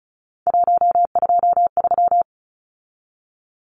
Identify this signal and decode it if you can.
Morse code, heard as a keyed tone, about 34 wpm, 700 Hz. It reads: 123
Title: Numbers - 3-Digit 35wpm